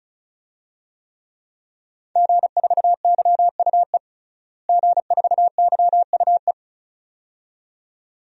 Morse code, heard as a keyed tone, about 34 wpm, 700 Hz. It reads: G4YUE G4YUE
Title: DXCC-100 Call Signs - 5 Characters - Repeated 2x - 35wpm